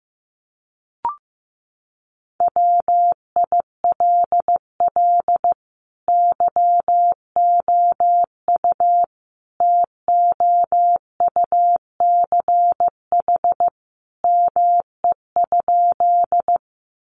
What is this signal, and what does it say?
Morse code, heard as a keyed tone, about 15 wpm, 700 Hz. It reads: WILL YOU TOUCH ME?
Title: Sentences from Top 1200 Words 15wpm